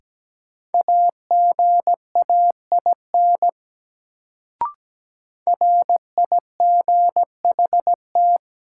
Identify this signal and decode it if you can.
Morse code, heard as a keyed tone, about 17 wpm, 700 Hz. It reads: AGAIN RIGHT
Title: Sentences from Top 200 Words - New Word Drill - 17wpm